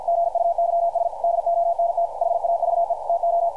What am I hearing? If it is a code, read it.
Morse code, heard as a keyed tone, about 50 wpm, 680 Hz, with radio noise. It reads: GUYS DONT HAVE TO
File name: archived / CRSnet